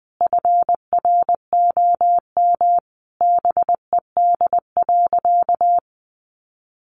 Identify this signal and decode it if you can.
Morse code, heard as a keyed tone, about 20 wpm, 700 Hz. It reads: FROM BED.
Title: Sentences from Top 900 Words 20wpm